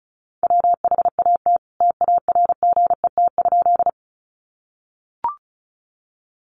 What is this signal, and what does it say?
Morse code, heard as a keyed tone, about 34 wpm, 700 Hz. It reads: WHAT TARGET?
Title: Sentences from Top 1200 Words 35wpm